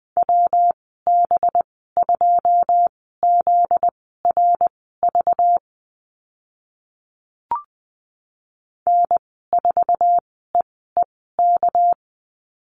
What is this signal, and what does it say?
Morse code, heard as a keyed tone, about 20 wpm, spaced as for 14 wpm, 700 Hz. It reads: WB2ZRV N4EEK
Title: 245 Lesson 39g - Letter z - Call Signs 20wpm